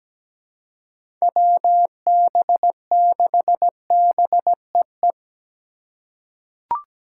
Morse code, heard as a keyed tone, about 17 wpm, 700 Hz. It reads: WB6BEE
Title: K1USN SST QSO Practice - 17wpm